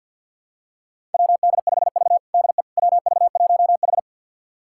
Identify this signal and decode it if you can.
Morse code, heard as a keyed tone, about 52 wpm, 700 Hz. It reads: WD5V DE WV1H
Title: USA Callsign DE Callsign - 4 Characters - 50wpm